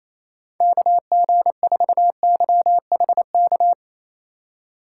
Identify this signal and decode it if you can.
Morse code, heard as a keyed tone, about 28 wpm, 700 Hz. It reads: KG4YHK
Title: USA Call Signs - 6 Characters - 4x Character Spacing - 28wpm